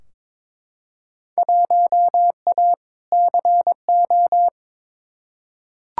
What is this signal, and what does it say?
Morse code, heard as a keyed tone, about 22 wpm, 700 Hz. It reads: 1A CO
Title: Field Day Exchanges - 22wpm